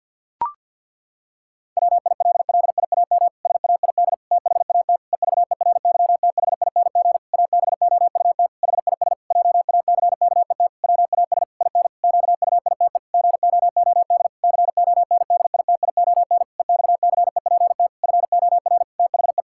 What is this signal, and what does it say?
Morse code, heard as a keyed tone, about 50 wpm, 700 Hz. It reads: WILLIAM SAID THAT EVERYTHING ABOUT HIS JACKET WAS IN QUITE GOOD CONDITION EXCEPT FOR THE
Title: Pangram Challenge v2 - 50wpm